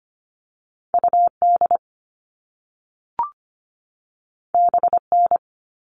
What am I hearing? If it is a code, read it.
Morse code, heard as a keyed tone, about 25 wpm, 700 Hz. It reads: UD BN
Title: All Cut Numbers - 2-Digit 25wpm